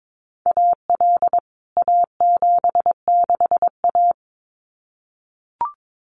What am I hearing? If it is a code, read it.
Morse code, heard as a keyed tone, about 22 wpm, 700 Hz. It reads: AL A76A
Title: CWT Exchange 22wpm